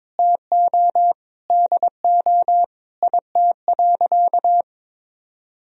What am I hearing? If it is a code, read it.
Morse code, heard as a keyed tone, about 22 wpm, 700 Hz. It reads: TO DO IT.
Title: Sentences from Top 1000 Words - Review Part 3 - 22wpm